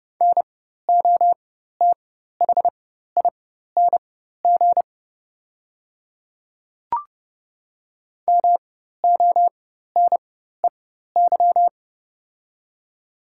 Morse code, heard as a keyed tone, about 30 wpm, spaced as for 14 wpm, 700 Hz. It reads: NOTHING MONEY
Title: 4x Character Spacing - Top 500 Words - 30wpm